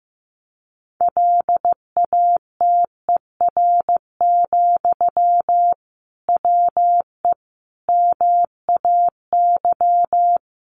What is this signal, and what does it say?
Morse code, heard as a keyed tone, about 14 wpm, 700 Hz. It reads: LATER, WE MAY
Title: Sentences from Top 800 Words 15wpm